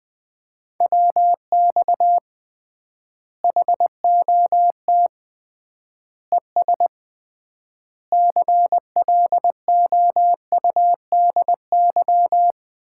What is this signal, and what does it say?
Morse code, heard as a keyed tone, about 20 wpm, 700 Hz. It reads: WX HOT ES CLOUDY
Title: Common QSO Phrases - 3x Word-Spacing - 20wpm